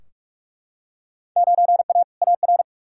Morse code, heard as a keyed tone, about 41 wpm, 700 Hz. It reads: 9A AR
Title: Field Day Exchanges Encore - 45wpm